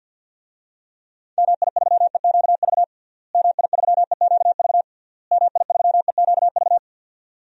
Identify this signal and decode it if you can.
Morse code, heard as a keyed tone, about 46 wpm, 700 Hz. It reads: MI3EQV MI3EQV MI3EQV
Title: DXCC-100 Call Signs - 6 Characters - Repeated 3x - 50wpm